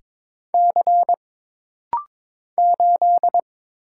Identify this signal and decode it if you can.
Morse code, heard as a keyed tone, about 22 wpm, 700 Hz. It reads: C 8
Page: This rapid-fire timing is geared to push your ICR (Instant Character Recognition) skills.